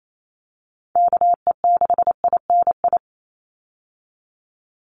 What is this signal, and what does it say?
Morse code, heard as a keyed tone, about 28 wpm, 700 Hz. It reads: KE6INI
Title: USA Call Signs 28wpm